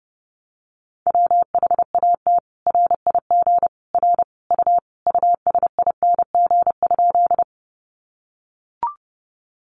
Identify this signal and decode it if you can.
Morse code, heard as a keyed tone, about 30 wpm, 700 Hz. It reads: WHAT RIG R U USING?
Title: Common QSO Phrases 30wpm